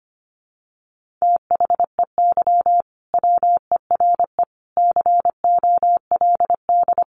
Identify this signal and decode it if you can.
Morse code, heard as a keyed tone, about 25 wpm, 700 Hz. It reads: THEY WERE COLD
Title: Sentences from Top 1500 Words 25wpm